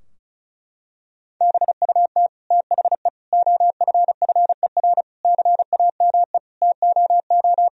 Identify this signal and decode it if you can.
Morse code, heard as a keyed tone, about 34 wpm, 700 Hz. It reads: BUT THE OFFER CAME TOO
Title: Sentences from Top 1000 Words - Review Part 4 - 35wpm